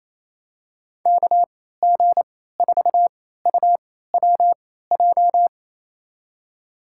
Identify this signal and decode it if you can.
Morse code, heard as a keyed tone, about 28 wpm, spaced as for 16 wpm, 700 Hz. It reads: KG4UWJ
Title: USA Call Signs - 6 Characters - 3x Character Spacing - 28wpm